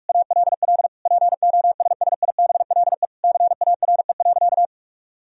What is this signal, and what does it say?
Morse code, heard as a keyed tone, about 46 wpm, 700 Hz. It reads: ALL POSSIBLE CARE.
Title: Sentences from Top 1000 Words - Review Part 2 - 45wpm